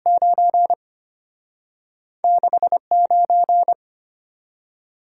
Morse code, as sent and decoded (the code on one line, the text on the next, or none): ----. / -.... / ----.
9 6 9